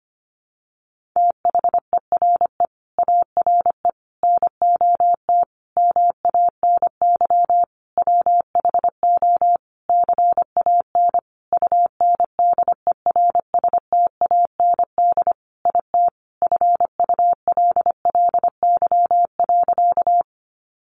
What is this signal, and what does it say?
Morse code, heard as a keyed tone, about 25 wpm, 700 Hz. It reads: THERE ARE NOT MANY WHO CAN UNDERSTAND IT FULLY.
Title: Sentences from Top 1300 Words 25wpm